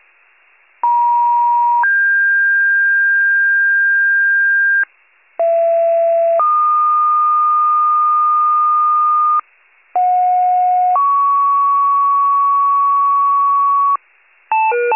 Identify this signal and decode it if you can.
Two-tone page, heard as a one-second tone, then a three-second tone, about 950 then 1600 Hz, about 670 then 1220 Hz, about 730 then 1120 Hz. It is heard from 0.8 to 14.0 s.
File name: wcalert